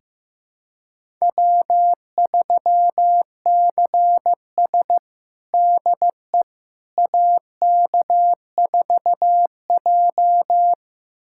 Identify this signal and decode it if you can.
Morse code, heard as a keyed tone, about 15 wpm, 700 Hz. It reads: W3CS DE AK4J
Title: USA Callsign DE Callsign - 4 Characters - 15wpm